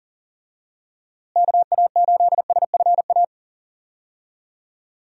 Morse code, heard as a keyed tone, about 40 wpm, 700 Hz. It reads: KA8SFA
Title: USA Call Signs - 6 Characters - Repeated 2x - 40wpm